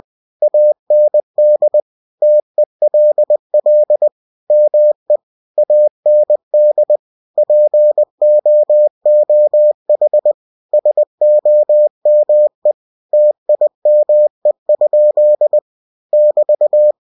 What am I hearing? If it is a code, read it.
Morse code, heard as a keyed tone, about 20 wpm, 600 Hz. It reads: AND TELL ME AND POOH SOME TIME? <BT>
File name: cw-books / winnie-the-pooh / winnie-20wpm-page-0110